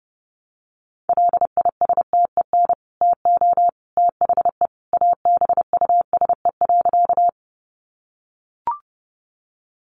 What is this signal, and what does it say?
Morse code, heard as a keyed tone, about 30 wpm, 700 Hz. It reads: LISTEN TO THE ABUSE.
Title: Sentences from Top 1300 Words 30wpm